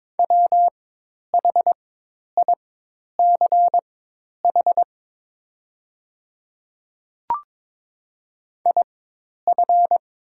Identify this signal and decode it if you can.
Morse code, heard as a keyed tone, about 22 wpm, spaced as for 10 wpm, 690 Hz. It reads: WHICH IF
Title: 4x Character Spacing - Top 100 Words - 22wpm